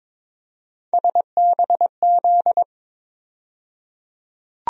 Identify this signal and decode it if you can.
Morse code, heard as a keyed tone, about 21 wpm, 700 Hz. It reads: SBZ
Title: USA Call Sign Postfixes - 3 Letters 22wpm